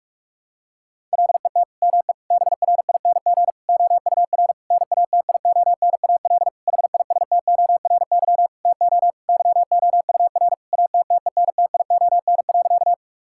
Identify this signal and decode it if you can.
Morse code, heard as a keyed tone, about 46 wpm, 700 Hz. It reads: LET ME BRING OUR NATIONAL HISTORY TO YOUR ATTENTION.
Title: Sentences from Top 1000 Words - Review Part 6 - 45wpm